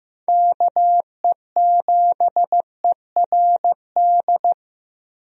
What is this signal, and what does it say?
Morse code, heard as a keyed tone, about 15 wpm, 700 Hz. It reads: KE7ERD
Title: USA Call Signs - 6 Characters - Repeated 2x - 15wpm